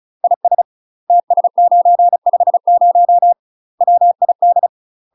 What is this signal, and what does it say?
Morse code, heard as a keyed tone, about 36 wpm, 700 Hz. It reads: IS TS950 WID
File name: UK MASTER QSOs TRAINING/35WPM